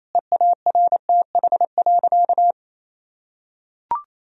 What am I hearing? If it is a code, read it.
Morse code, heard as a keyed tone, about 28 wpm, 700 Hz. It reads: EARTH.
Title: Sentences from Top 1000 Words - Review Part 2 - 28wpm